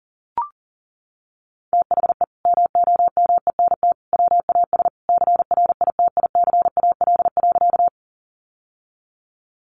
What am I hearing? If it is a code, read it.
Morse code, heard as a keyed tone, about 40 wpm, 700 Hz. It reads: THE MOMENT WAS CRITICAL.
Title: Sentences from Top 1100 Words 40wpm